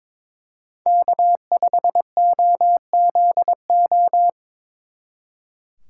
Morse code, heard as a keyed tone, about 24 wpm, 700 Hz. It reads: K5OZO
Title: Speed Racing - USA 1x3 Call Signs 22wpm